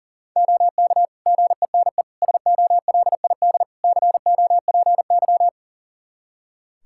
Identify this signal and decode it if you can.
Morse code, heard as a keyed tone, about 40 wpm, 700 Hz. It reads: OK GENE SOLID COPY
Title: Practice QSO - Exchange 3 - More Info - 40wpm